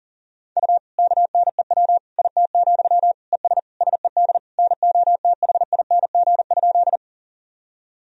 Morse code, heard as a keyed tone, about 40 wpm, 700 Hz. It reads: U KNEW IT, ES SED NOTHING?
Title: Sentences from Top 500 Words and Abbreviations 40wpm